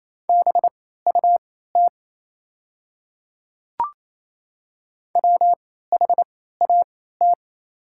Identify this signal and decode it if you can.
Morse code, heard as a keyed tone, about 27 wpm, spaced as for 16 wpm, 690 Hz. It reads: BUT WHAT
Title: 3x Character Spacing - Top 100 Words - 28wpm